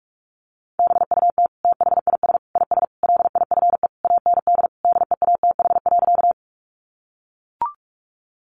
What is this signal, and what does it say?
Morse code, heard as a keyed tone, about 44 wpm, 700 Hz. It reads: BUT THIS IS LIFE AND DEATH.
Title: Sentences from Top 1100 Words 45wpm